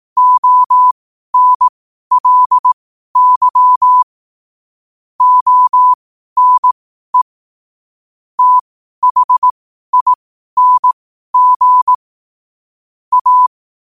Morse code, heard as a keyed tone, about 18 wpm, spaced as for 12 wpm, 1000 Hz. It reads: ONLY ONE THING A
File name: qotdfiles / 12wpm